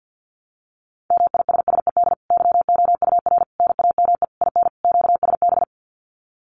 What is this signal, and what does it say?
Morse code, heard as a keyed tone, about 50 wpm, 700 Hz. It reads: MISSED YOUR NAME IN QSB
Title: Common QSO Phrases 50wpm